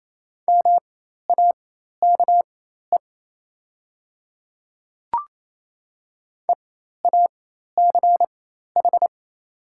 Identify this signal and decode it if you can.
Morse code, heard as a keyed tone, about 28 wpm, spaced as for 13 wpm, 700 Hz. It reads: MAKE EACH
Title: 4x Character Spacing - Top 300 Words - 28wpm